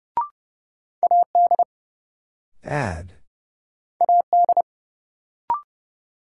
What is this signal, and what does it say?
Morse code, heard as a keyed tone, about 30 wpm, 700 Hz. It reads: AD AD
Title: Sentences from Top 1300 Words - New Word Drill Alt - 30wpm